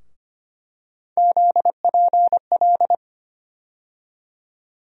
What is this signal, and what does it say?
Morse code, heard as a keyed tone, about 25 wpm, 700 Hz. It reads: ZPL
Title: USA Call Sign Postfixes - 3 Letters 25wpm